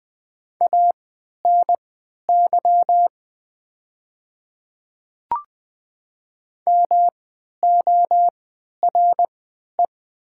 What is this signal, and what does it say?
Morse code, heard as a keyed tone, about 20 wpm, spaced as for 11 wpm, 700 Hz. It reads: ANY MORE
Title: 3x Character Spacing - Top 300 Words - 20wpm